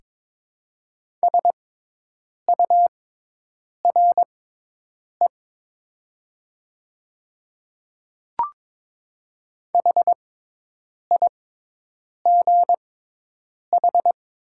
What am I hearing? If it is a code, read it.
Morse code, heard as a keyed tone, about 22 wpm, spaced as for 8 wpm, 700 Hz. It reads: SURE HIGH
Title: 6x Character Spacing - Top 500 Words - 22wpm